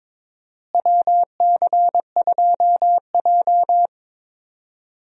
Morse code, heard as a keyed tone, about 22 wpm, 700 Hz. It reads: WC2J
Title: USA Call Signs - 4 Characters - Repeated 3x - 22wpm